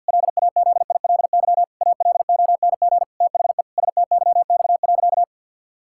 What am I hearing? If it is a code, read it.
Morse code, heard as a keyed tone, about 50 wpm, 700 Hz. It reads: LAZILY ALONG THE STYX.
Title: Pangram Challenge v2 - 50wpm